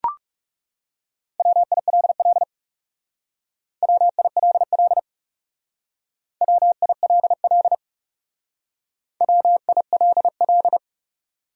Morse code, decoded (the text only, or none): WILL WILL WILL WILL